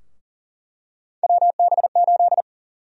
Morse code, heard as a keyed tone, about 38 wpm, 700 Hz. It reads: WB8
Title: USA Call Sign Prefixes - Repeated 2x - 40wpm